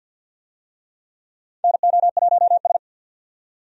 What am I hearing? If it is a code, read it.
Morse code, heard as a keyed tone, about 47 wpm, 700 Hz. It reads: NO1S
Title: USA Call Signs - 4 Characters - 50wpm